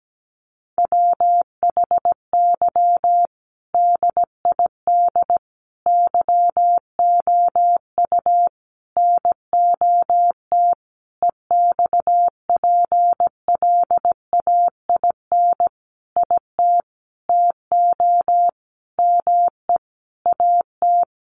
Morse code, heard as a keyed tone, about 17 wpm, 700 Hz. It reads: WHY DID YOU NOT EXPLAIN IT TO ME AT
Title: Sentences from Top 1100 Words 17wpm